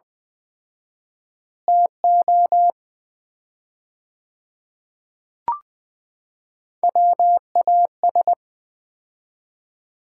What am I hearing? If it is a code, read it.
Morse code, heard as a keyed tone, about 20 wpm, 700 Hz. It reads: TO WAS
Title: 4x Word-Spacing - Sets of 5 Words - 20wpm